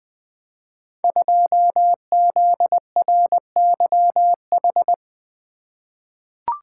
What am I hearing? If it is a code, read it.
Morse code, heard as a keyed tone, about 21 wpm, 700 Hz. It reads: 2ZRYH
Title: Code Groups - 5 Letter:Numbers - 20wpm